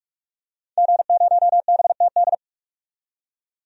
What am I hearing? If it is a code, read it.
Morse code, heard as a keyed tone, about 46 wpm, 700 Hz. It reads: G0BTD
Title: DXCC-100 Call Signs - 5 Characters - 45wpm